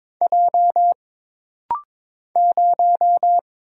Morse code, heard as a keyed tone, about 22 wpm, 700 Hz. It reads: J 0
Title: Single Letter-Number - Rapid-Fire 22wpm